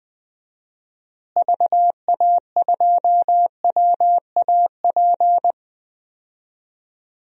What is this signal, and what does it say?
Morse code, heard as a keyed tone, about 21 wpm, 700 Hz. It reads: VA2WAP
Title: DXCC-100 Call Signs - 6 Characters - Repeated 2x - 20wpm